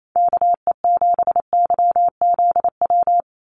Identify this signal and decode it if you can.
Morse code, heard as a keyed tone, about 28 wpm, 700 Hz. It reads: KE7YZW
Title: USA Call Signs 28wpm